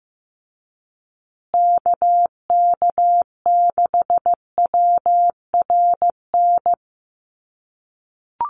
Morse code, heard as a keyed tone, about 15 wpm, 700 Hz. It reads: KK6WRN
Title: USA Call Signs Repeated 2x 15wpm